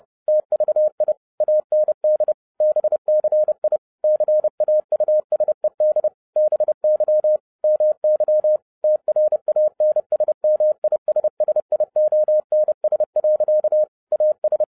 Morse code, heard as a keyed tone, about 30 wpm, 610 Hz. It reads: TVI AND BCI CAUSED BY MY TRANSMISSIONS. AS
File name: 2011-07-mp3 / n40 EXCITING QRP CW OPERATION 30wpm